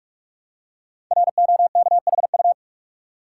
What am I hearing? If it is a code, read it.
Morse code, heard as a keyed tone, about 44 wpm, 700 Hz. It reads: ROKHU
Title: Code Groups - 5 Letters - 45wpm